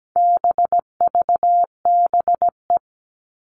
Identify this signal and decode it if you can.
Morse code, heard as a keyed tone, about 17 wpm, 700 Hz. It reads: BVBE
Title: All Cut Numbers - 4-Digit 17wpm